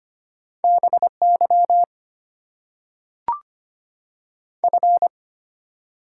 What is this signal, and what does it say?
Morse code, heard as a keyed tone, about 25 wpm, 700 Hz. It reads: BY F
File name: Top 90 QSO Elements - 8x Character Spacing - 25wpm